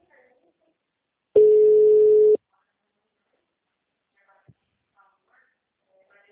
Ringback tone (Europe, 425 Hz).